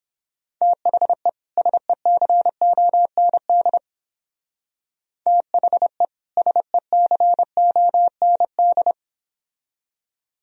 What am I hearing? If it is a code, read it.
Morse code, heard as a keyed tone, about 28 wpm, 700 Hz. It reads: THE SECOND THE SECOND
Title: Speed-Racing - Sets of 2 Words 22wpm